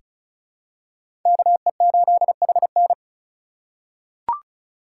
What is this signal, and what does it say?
Morse code, heard as a keyed tone, about 36 wpm, 700 Hz. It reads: KE8HN
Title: USA Call Signs - 5 Characters - 4x Character Spacing - 35wpm